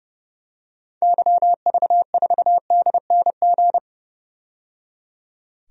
Morse code, heard as a keyed tone, about 30 wpm, 700 Hz. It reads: YV4DNG
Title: DXCC-100 Call Signs - 6 Characters - 30wpm